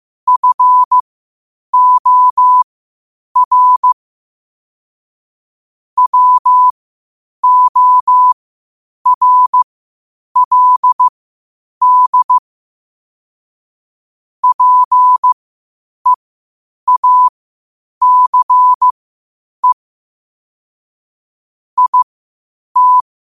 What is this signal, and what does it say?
Morse code, heard as a keyed tone, about 15 wpm, spaced as for 8 wpm, 1000 Hz. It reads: FOR WORLD PEACE IT
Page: 8 wpm morse code quotes